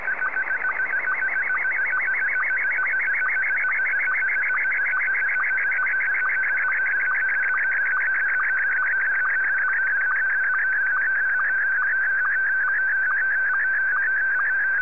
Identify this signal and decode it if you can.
SSTV (slow-scan television), Scottie 1.